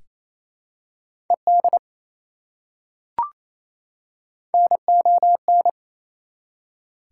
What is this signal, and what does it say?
Morse code, heard as a keyed tone, about 28 wpm, 700 Hz. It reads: ED NON
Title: Prefixes and Suffixes 28wpm